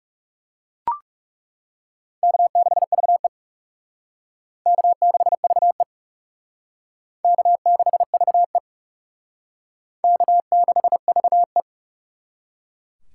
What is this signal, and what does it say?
Morse code, heard as a keyed tone, about 37 wpm, 700 Hz. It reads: K6VE K6VE K6VE K6VE
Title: Speed Racing - USA 1x2 Call Signs 30wpm